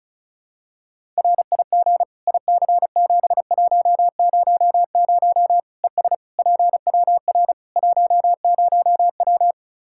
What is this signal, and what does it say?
Morse code, heard as a keyed tone, about 36 wpm, 700 Hz. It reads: RIG IC7100 ES PWR 10W
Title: K1USN SST QSO Practice 35wpm